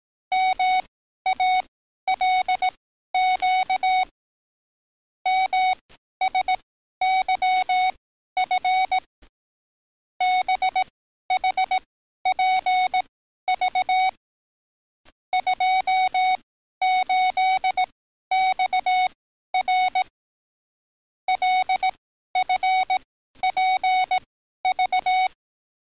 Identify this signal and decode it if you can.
Morse code, heard as a keyed tone, about 18 wpm, spaced as for 12 wpm, 740 Hz. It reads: MALQ MSYF BHPV 28XR LFPV
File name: training / NAVPERS 11049to11060 / NAVPERS 11053 RA RB